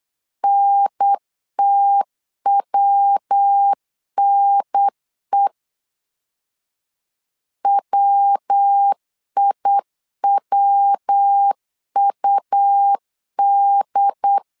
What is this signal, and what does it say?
Morse code, heard as a keyed tone, about 8 wpm, 800 Hz. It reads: NTWNE WIWUD
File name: LEZIONI / 3 MWRO / 3 MW 8wpm 2501